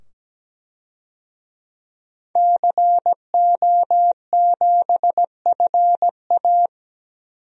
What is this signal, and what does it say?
Morse code, heard as a keyed tone, about 17 wpm, 700 Hz. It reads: CO7FA
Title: DXCC-100 Call Signs - 5 Characters - 17wpm